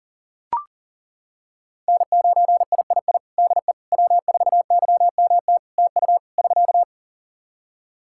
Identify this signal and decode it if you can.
Morse code, heard as a keyed tone, about 40 wpm, 700 Hz. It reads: N9III DE W4YMT TU <SK>
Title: Practice QSO - Exchange 4 - Ending - 40wpm